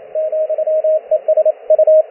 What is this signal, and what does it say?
Morse code, heard as a keyed tone, about 28 wpm, 620 Hz, with radio noise. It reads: ,ES U